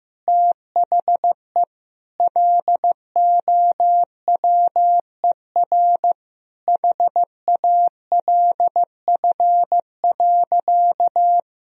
Morse code, heard as a keyed tone, about 16 wpm, 700 Hz. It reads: THE LOWER HALF.
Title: Sentences from Top 1000 Words - Review Part 4 - 15wpm